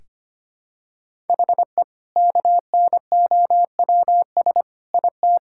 Morse code, heard as a keyed tone, about 25 wpm, 700 Hz. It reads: HE KNOWS IT
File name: practice / 25wpm / Sentences with 2-Word Phrases - 25wpm